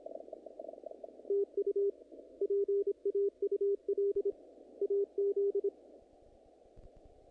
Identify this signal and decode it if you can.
Morse code, heard as a keyed tone, about 26 wpm, 410 Hz. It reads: TU PAUL AZ
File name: contests / naqp-cw-k5tr-op-k5pi- -1900